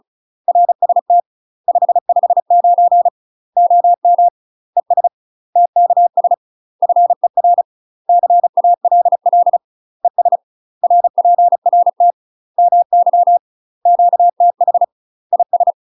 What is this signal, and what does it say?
Morse code, heard as a keyed tone, about 34 wpm, 700 Hz. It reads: RST 559 OM ES TKS FER CALL ES RPRT MY QTH IS
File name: UK MASTER QSOs TRAINING / 35WPM / UK MASTER QSOs 0003